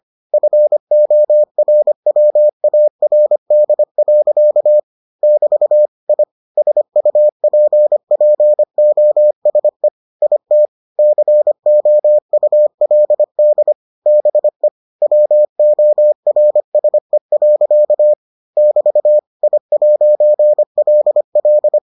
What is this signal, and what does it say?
Morse code, heard as a keyed tone, about 25 wpm, 600 Hz. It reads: FORWARD. <BT> I SUPPOSE IT COULD BE WORSE. <BT> I'LL